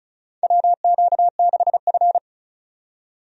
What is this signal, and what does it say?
Morse code, heard as a keyed tone, about 36 wpm, 700 Hz. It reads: WQ6F
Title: USA Call Signs - 4 Characters - 35wpm